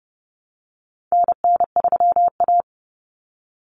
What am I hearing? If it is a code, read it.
Morse code, heard as a keyed tone, about 30 wpm, 700 Hz. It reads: NN3A
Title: Common Cut Numbers - 4-Digit 30wpm